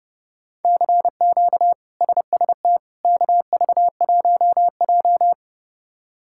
Morse code, heard as a keyed tone, about 30 wpm, 700 Hz. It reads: CQ SST KV1J
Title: K1USN SST QSO Practice - 30wpm